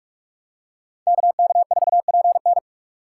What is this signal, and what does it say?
Morse code, heard as a keyed tone, about 41 wpm, 700 Hz. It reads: KK4PN